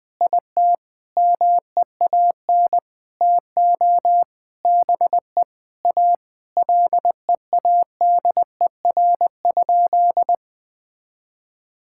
Morse code, heard as a keyed tone, about 19 wpm, 700 Hz. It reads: IT MEAN TO BE A LEADER?
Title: Sentences with 7-Word Phrases - 20wpm